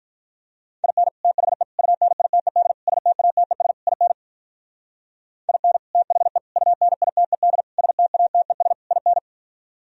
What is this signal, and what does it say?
Morse code, heard as a keyed tone, about 53 wpm, 700 Hz. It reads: IN THE UNITED STATES IN IN THE UNITED STATES IN